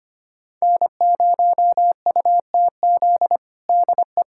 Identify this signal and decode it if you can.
Morse code, heard as a keyed tone, about 25 wpm, 700 Hz. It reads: N0UTZ DE
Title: USA Callsign DE Callsign - 5 Characters - 25wpm